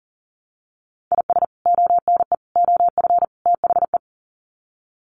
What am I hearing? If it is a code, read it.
Morse code, heard as a keyed tone, about 40 wpm, 700 Hz. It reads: IS ONE OF THE
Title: Sets of 4 Words 40wpm